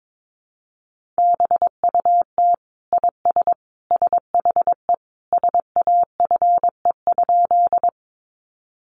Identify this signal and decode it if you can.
Morse code, heard as a keyed tone, about 22 wpm, 700 Hz. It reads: BUT IS SHE SAFE?
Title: Sentences from Top 900 Words 22wpm